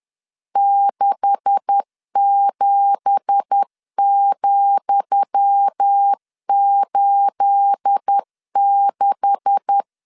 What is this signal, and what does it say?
Morse code, heard as a keyed tone, about 10 wpm, 800 Hz. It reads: 67,86
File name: LEZIONI / 10 SIGNS / 10D COMA 10wpm 3013